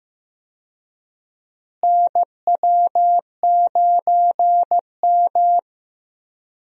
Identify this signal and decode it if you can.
Morse code, heard as a keyed tone, about 15 wpm, 700 Hz. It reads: NW9M
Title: USA Call Signs - 4 Characters - Repeated 2x - 15wpm